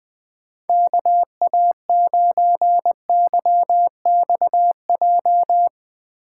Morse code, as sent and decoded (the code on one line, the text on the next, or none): -.- .- ----. -.-- -..- .---
KA9YXJ